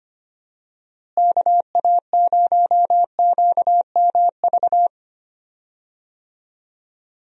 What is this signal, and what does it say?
Morse code, heard as a keyed tone, about 25 wpm, 700 Hz. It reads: KA0QMV
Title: USA Call Signs - 6 Characters - Repeated 3x - 25wpm